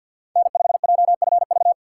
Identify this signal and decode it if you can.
Morse code, heard as a keyed tone, about 52 wpm, 700 Hz. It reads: N5JFV
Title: USA Call Signs - 4x Character Spacing - 50wpm